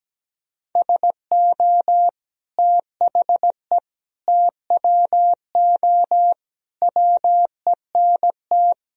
Morse code, heard as a keyed tone, about 17 wpm, 700 Hz. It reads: SO THE TWO WENT
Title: Sentences from Top 1000 Words - Review Part 1 - 17wpm